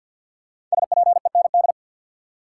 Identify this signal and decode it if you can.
Morse code, heard as a keyed tone, about 53 wpm, 700 Hz. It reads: SPEND
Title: Five Letter Words - 50wpm